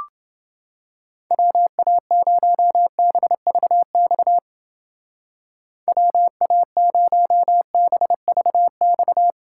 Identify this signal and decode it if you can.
Morse code, heard as a keyed tone, about 28 wpm, 700 Hz. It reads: WA0BVX WA0BVX